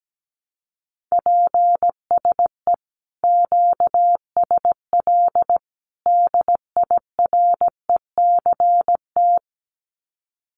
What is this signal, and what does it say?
Morse code, heard as a keyed tone, about 17 wpm, 700 Hz. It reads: PSE QSL DIRECT
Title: Common QSO Phrases 17wpm